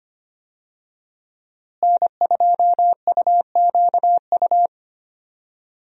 Morse code, heard as a keyed tone, about 25 wpm, 700 Hz. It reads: N2UQU
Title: USA Call Signs - 5 Characters - Repeated 2x - 25wpm